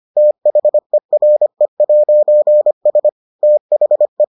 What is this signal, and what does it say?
Morse code, heard as a keyed tone, about 25 wpm, 600 Hz. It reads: THERE'S THE